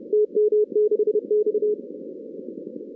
Morse code, heard as a keyed tone, about 31 wpm, 410 Hz. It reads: TM6X
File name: 2016 arrl dx cw / 2016-ARRL-DX-CW-K5TR- - 0-46